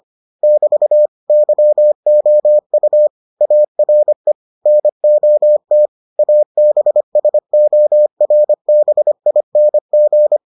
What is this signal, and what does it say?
Morse code, heard as a keyed tone, about 25 wpm, 600 Hz. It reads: <BT> YOU ARE NOT ABSORBING